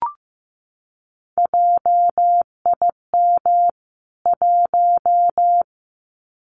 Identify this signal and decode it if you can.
Morse code, heard as a keyed tone, about 15 wpm, 700 Hz. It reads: JIM 1
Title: CWT Exchange 15wpm